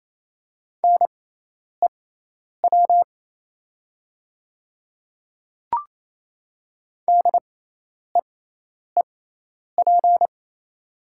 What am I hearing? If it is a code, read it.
Morse code, heard as a keyed tone, about 28 wpm, spaced as for 10 wpm, 700 Hz. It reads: NEW DEEP
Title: 6x Character Spacing - Top 500 Words - 28wpm